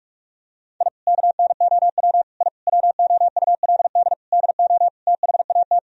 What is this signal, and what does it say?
Morse code, heard as a keyed tone, about 46 wpm, 700 Hz. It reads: I KNOW I WOULD DO THAT
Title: Sentences from Top 1000 Words - Review Part 3 - 45wpm